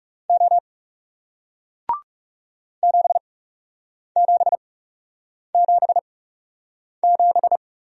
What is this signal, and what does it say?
Morse code, heard as a keyed tone, about 40 wpm, spaced as for 8 wpm, 700 Hz. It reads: O 7777